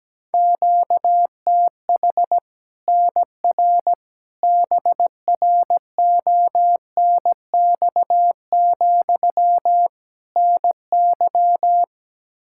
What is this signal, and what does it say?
Morse code, heard as a keyed tone, about 17 wpm, 700 Hz. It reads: QTH NR BRONX, NY
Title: Practice QSO - Exchange 1 - Beginning - 17wpm